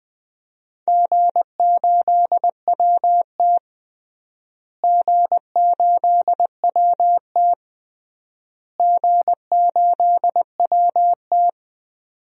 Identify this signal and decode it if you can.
Morse code, heard as a keyed tone, about 20 wpm, 700 Hz. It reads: G8WT G8WT G8WT
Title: DXCC-100 Call Signs - 4 Characters - Repeated 3x - 20wpm